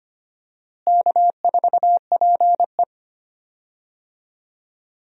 Morse code, heard as a keyed tone, about 25 wpm, 700 Hz. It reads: K4PE
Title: USA Call Signs - 4 Characters - Repeated 3x - 25wpm